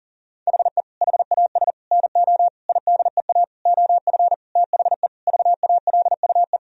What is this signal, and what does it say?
Morse code, heard as a keyed tone, about 40 wpm, 700 Hz. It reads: HE HAS NO IDEA OF THE VALUE
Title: Sentences from Top 1000 Words - Review Part 3 - 40wpm